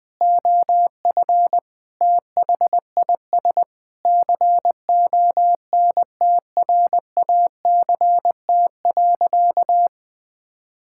Morse code, heard as a keyed tone, about 21 wpm, 700 Hz. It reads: OF THIS CONTRACT.
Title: Sentences with 9-Word Phrases - 20wpm